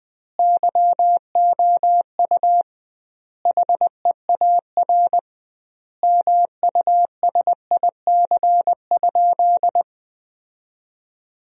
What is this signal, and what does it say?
Morse code, heard as a keyed tone, about 21 wpm, 700 Hz. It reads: YOU HEAR MUSIC?
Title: 2x Word-Spacing - Sentences from Top 300 Words - 20wpm